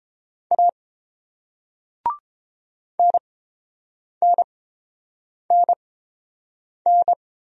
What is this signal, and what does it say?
Morse code, heard as a keyed tone, about 30 wpm, spaced as for 7 wpm, 690 Hz. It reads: A NNNN